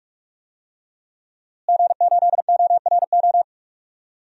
Morse code, heard as a keyed tone, about 46 wpm, 700 Hz. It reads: G8ORO
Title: DXCC-100 Call Signs - 5 Characters - Repeated 3x - 45wpm